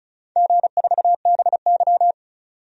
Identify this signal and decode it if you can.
Morse code, heard as a keyed tone, about 36 wpm, 700 Hz. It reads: G4BY
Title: DXCC-100 Call Signs - 4 Characters - 35wpm